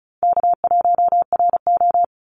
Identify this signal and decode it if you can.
Morse code, heard as a keyed tone, about 34 wpm, 700 Hz. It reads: K1RO
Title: Easy Call Signs Repeated 3x - 35wpm